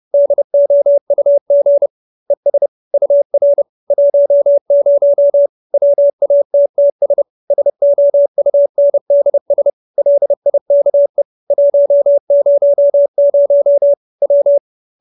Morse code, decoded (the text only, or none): DOUG ES UR 10 WATTS SOUNDS LIKE 100 W